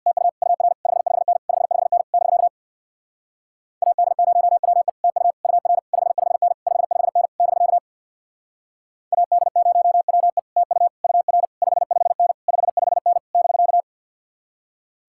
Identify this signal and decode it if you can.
Morse code, heard as a keyed tone, about 66 wpm, 700 Hz. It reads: TU UR 55N 55N <BK> AD0WE TU UR 55N 55N <BK> AD0WE TU UR 55N 55N <BK>